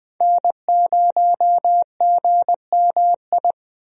Morse code, heard as a keyed tone, about 20 wpm, 700 Hz. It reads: N0GMI
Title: USA Call Signs - 5 Characters - 20wpm